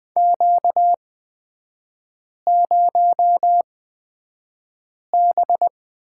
Morse code, heard as a keyed tone, about 20 wpm, spaced as for 5 wpm, 700 Hz. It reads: Q0B
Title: 236 Lesson 38e - Letter q - Three Characters 20wpm